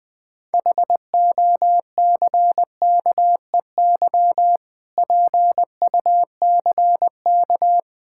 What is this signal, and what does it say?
Morse code, heard as a keyed tone, about 21 wpm, 700 Hz. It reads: HOCKEY PUCK
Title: Difficult Words with Hints - 20wpm